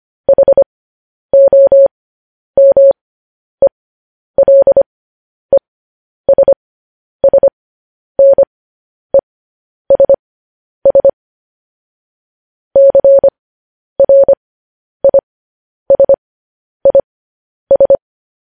Morse code, heard as a keyed tone, about 25 wpm, spaced as for 10 wpm, 550 Hz. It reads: HOMELESSNESS CRISIS
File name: podcast / News headlines-1x-25x10